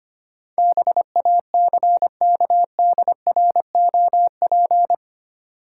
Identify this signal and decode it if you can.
Morse code, heard as a keyed tone, about 25 wpm, 700 Hz. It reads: BACKDROP
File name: 8 Letter Words - First 3 Letters Given - 25wpm